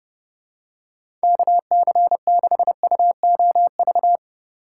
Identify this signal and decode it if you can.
Morse code, heard as a keyed tone, about 30 wpm, 700 Hz. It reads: KC6UOV
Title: USA Call Signs - 6 Characters - 6x Character Spacing - 30wpm